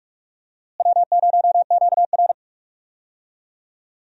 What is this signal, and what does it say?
Morse code, heard as a keyed tone, about 43 wpm, 700 Hz. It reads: W0QR
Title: USA Call Signs - 4 Characters - Repeated 2x - 45wpm